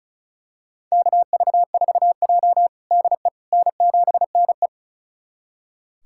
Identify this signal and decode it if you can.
Morse code, heard as a keyed tone, about 36 wpm, 700 Hz. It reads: KV4J DE N7NE
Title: USA Callsign DE Callsign - 4 Characters - 35wpm